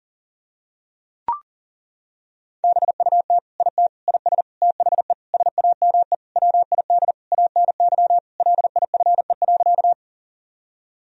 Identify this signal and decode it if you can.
Morse code, heard as a keyed tone, about 40 wpm, 700 Hz. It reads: BUT IT IS THE SAME WID ANY LIFE.
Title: Sentences from Top 300 Words and Abbreviations 40wpm